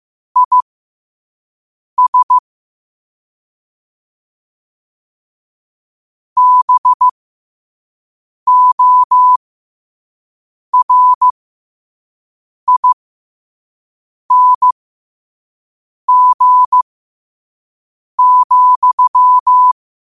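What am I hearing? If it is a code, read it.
Morse code, heard as a keyed tone, about 15 wpm, spaced as for 5 wpm, 1000 Hz. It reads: IS BORING,
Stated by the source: Morse quote-of-the-day practice at 5 wpm.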